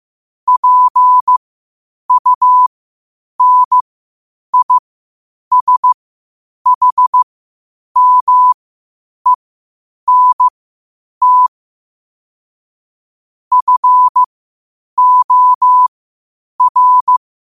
Morse code, heard as a keyed tone, about 15 wpm, spaced as for 8 wpm, 1000 Hz. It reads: PUNISHMENT FOR
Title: 8 WPM morse code quotes for Thu, 29 Feb 2024